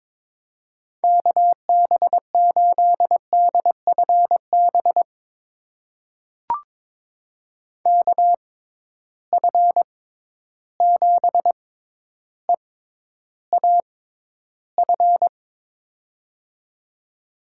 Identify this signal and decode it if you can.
Morse code, heard as a keyed tone, about 22 wpm, spaced as for 11 wpm, 700 Hz. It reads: KB8DFB KF7EAF
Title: USA Call Signs - 6 Characters - 6x Character Spacing - 22wpm